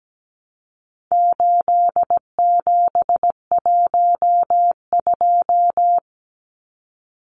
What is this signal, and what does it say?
Morse code, heard as a keyed tone, about 17 wpm, 700 Hz. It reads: 8712
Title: Numbers - 4-Digit 17wpm